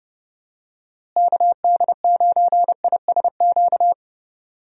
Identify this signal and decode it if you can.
Morse code, heard as a keyed tone, about 30 wpm, 700 Hz. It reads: KD9ISQ
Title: USA Call Signs - 6 Characters - Repeated 3x - 30wpm